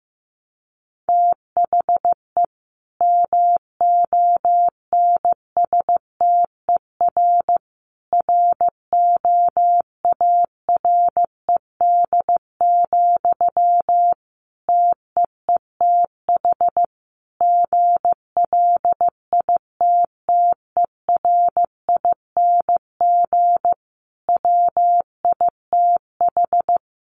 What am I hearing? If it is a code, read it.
Morse code, heard as a keyed tone, about 15 wpm, 700 Hz. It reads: THE MONSTER ROARED, TEETH GLITTERING WITH
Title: A Sound Of Thunder 15wpm